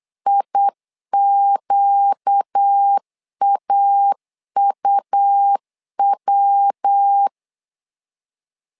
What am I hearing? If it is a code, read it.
Morse code, heard as a keyed tone, about 8 wpm, 800 Hz. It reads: IQAUW
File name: 4 Q 8wpm